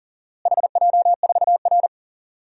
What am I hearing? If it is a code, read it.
Morse code, heard as a keyed tone, about 40 wpm, 700 Hz. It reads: HJ4R
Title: DXCC-100 Call Signs - 4 Characters - Repeated 2x - 40wpm